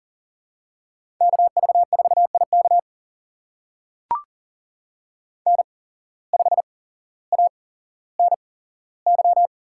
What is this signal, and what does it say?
Morse code, heard as a keyed tone, about 38 wpm, spaced as for 17 wpm, 700 Hz. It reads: KV4IK N5ANY
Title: USA Call Signs - 5 Characters - 8x Character Spacing - 40wpm